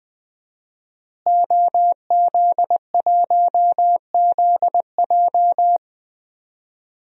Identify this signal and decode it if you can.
Morse code, heard as a keyed tone, about 20 wpm, 700 Hz. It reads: OZ1ZJ
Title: DXCC-100 Call Signs - 5 Characters - Repeated 3x - 20wpm